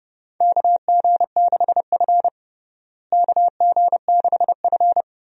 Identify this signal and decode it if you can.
Morse code, heard as a keyed tone, about 30 wpm, 700 Hz. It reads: KG6F KG6F
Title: USA Call Signs - 4 Characters - Repeated 2x - 30wpm